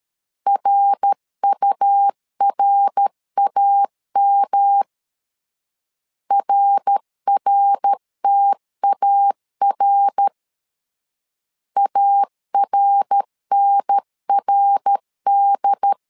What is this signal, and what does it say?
Morse code, heard as a keyed tone, about 13 wpm, 800 Hz. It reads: RURAM RRTAR ARNRD
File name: LEZIONI / 3 MWRO / 3 MWR 12wpm 1711